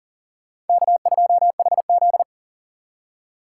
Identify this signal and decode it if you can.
Morse code, heard as a keyed tone, about 51 wpm, 700 Hz. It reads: K2HZ